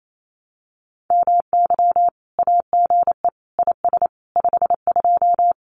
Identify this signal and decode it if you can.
Morse code, heard as a keyed tone, about 28 wpm, 700 Hz. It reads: MY AGE IS 52
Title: Common QSO Phrases Encore 28wpm